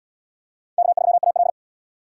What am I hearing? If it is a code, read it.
Morse code, heard as a keyed tone, about 77 wpm, 690 Hz. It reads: K3AL